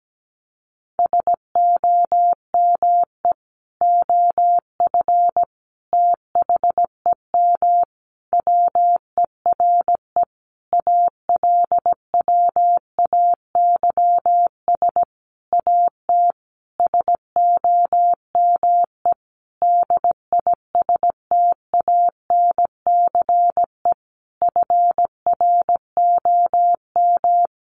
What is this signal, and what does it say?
Morse code, heard as a keyed tone, about 17 wpm, 700 Hz. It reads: SOME OF THEM WERE ALWAYS AT SOME DISTANCE FROM
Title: Sentences from Top 1200 Words 17wpm